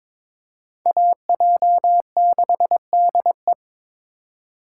Morse code, heard as a keyed tone, about 25 wpm, 700 Hz. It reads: AJ6DE
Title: Speed Racing - USA Call Signs 22wpm